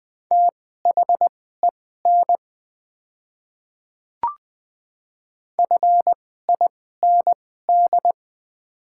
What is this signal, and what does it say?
Morse code, heard as a keyed tone, about 20 wpm, spaced as for 14 wpm, 700 Hz. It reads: THEN FIND
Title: 2x Character Spacing - Top 200 Words - 20wpm